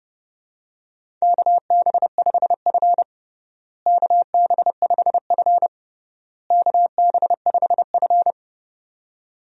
Morse code, heard as a keyed tone, about 31 wpm, 700 Hz. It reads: KB5F KB5F KB5F
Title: USA Call Signs - 4 Characters - Repeated 3x - 30wpm